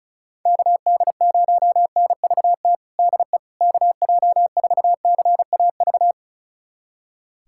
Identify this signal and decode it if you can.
Morse code, heard as a keyed tone, about 36 wpm, 700 Hz. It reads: KD0NVT DE KJ4CAV
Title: USA Callsign DE Callsign 35wpm